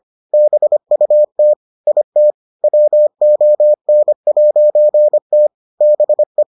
Morse code, heard as a keyed tone, about 25 wpm, 600 Hz. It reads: BUT IT WON'T BE